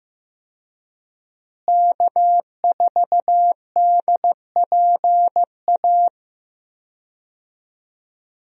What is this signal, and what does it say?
Morse code, heard as a keyed tone, about 15 wpm, 700 Hz. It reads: K4DPA
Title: USA Call Signs - 5 Characters - Repeated 2x - 15wpm